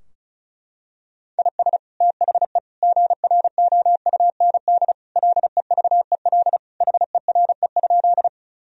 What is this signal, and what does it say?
Morse code, heard as a keyed tone, about 36 wpm, 700 Hz. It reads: IS THE GROUND LEVEL HERE?
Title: Sentences from Top 1000 Words - Review Part 4 - 35wpm